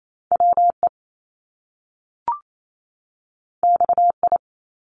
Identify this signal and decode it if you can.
Morse code, heard as a keyed tone, about 28 wpm, 700 Hz. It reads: WE XI
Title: Top Two Letter Combinations in Words 28wpm